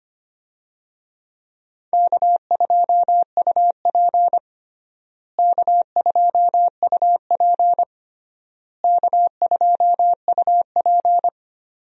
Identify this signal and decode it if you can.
Morse code, heard as a keyed tone, about 25 wpm, 700 Hz. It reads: K2UP K2UP K2UP
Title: USA Call Signs - 4 Characters - Repeated 3x - 25wpm